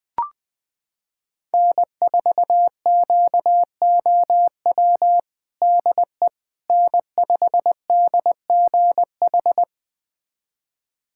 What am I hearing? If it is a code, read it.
Morse code, heard as a keyed tone, about 20 wpm, 700 Hz. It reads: N4QOW DE N5DGH
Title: USA Callsign DE Callsign - 5 Characters - 20wpm